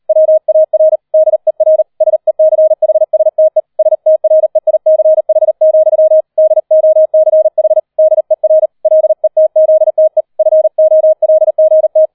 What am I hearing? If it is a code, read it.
Morse code, heard as a keyed tone, about 38 wpm, 630 Hz. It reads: WAR DER SECHSTE STREICH, DOCH DER LETZTE FOLGT